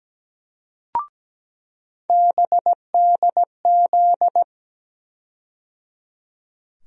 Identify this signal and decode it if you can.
Morse code, heard as a keyed tone, about 17 wpm, 700 Hz. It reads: BDZ
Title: USA Call Sign Postfixes - 3 Letters 17wpm